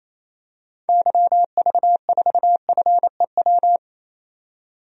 Morse code, heard as a keyed tone, about 28 wpm, 700 Hz. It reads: YV4FEW
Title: DXCC-100 Call Signs - 6 Characters - 28wpm